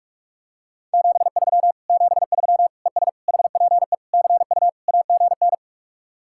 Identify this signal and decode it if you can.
Morse code, heard as a keyed tone, about 36 wpm, 700 Hz. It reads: 73 73 ES HPE CU AGN
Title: Speed-Racing Common QSO Phrases - 30wpm